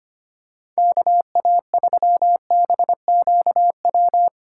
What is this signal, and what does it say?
Morse code, heard as a keyed tone, about 25 wpm, 700 Hz. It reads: KA3BQW
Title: USA Call Signs - 6 Characters - 8x Character Spacing - 25wpm